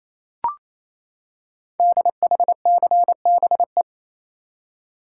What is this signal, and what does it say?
Morse code, heard as a keyed tone, about 27 wpm, 700 Hz. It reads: DHCBE
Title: Code Groups - 5 Letters - 28wpm